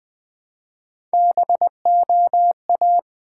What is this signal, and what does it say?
Morse code, heard as a keyed tone, about 20 wpm, 700 Hz. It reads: BOA
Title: Build A Word - 1 Letter At A Time - 5-Letter Words - 20wpm